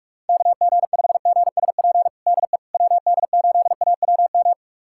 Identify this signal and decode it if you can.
Morse code, heard as a keyed tone, about 41 wpm, 700 Hz. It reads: KG5GSP DE WD8AWM
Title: USA Callsign DE Callsign 45wpm